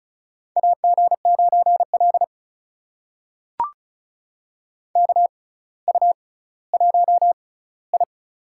Morse code, heard as a keyed tone, about 35 wpm, spaced as for 18 wpm, 700 Hz. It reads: AG9L KU1I
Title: USA Call Signs - 4 Characters - 6x Character Spacing - 35wpm